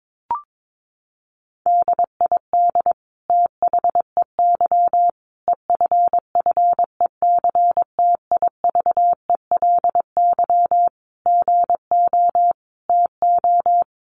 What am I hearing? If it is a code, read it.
Morse code, heard as a keyed tone, about 22 wpm, 700 Hz. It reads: DID THEY EFFECTIVELY GO TO
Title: Sentences from Top 1400 Words 22wpm